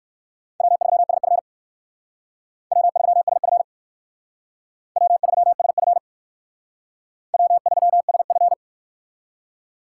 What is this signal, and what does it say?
Morse code, heard as a keyed tone, about 56 wpm, 700 Hz. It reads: W3SF W3SF W3SF W3SF